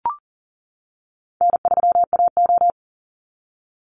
Morse code, heard as a keyed tone, about 40 wpm, 700 Hz. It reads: N3AO
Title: Call Signs - 40wpm